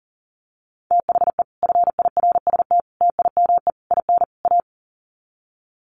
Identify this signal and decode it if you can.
Morse code, heard as a keyed tone, about 40 wpm, 700 Hz. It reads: THE FIRST TIME IN A
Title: Sets of 5 Words Encore 40wpm